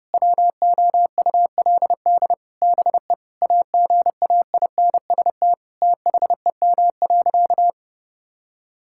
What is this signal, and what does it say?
Morse code, heard as a keyed tone, about 30 wpm, 700 Hz. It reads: WOULD BE AGAINST THEM.
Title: Sentences from Top 1000 Words - Review Part 6 - 30wpm